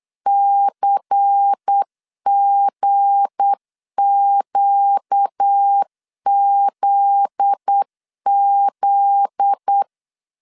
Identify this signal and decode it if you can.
Morse code, heard as a keyed tone, about 8 wpm, 800 Hz. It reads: CGQZZ
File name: LEZIONI / 7 GF9Z / 7 GF9Z 8wpm 3213